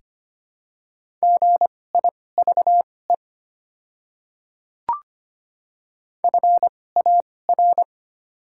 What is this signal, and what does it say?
Morse code, heard as a keyed tone, about 25 wpm, spaced as for 18 wpm, 690 Hz. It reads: GIVE FAR
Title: 2x Character Spacing - Top 300 Words - 25wpm